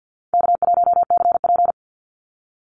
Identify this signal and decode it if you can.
Morse code, heard as a keyed tone, about 50 wpm, 700 Hz. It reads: K1CP
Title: Call Signs - 50wpm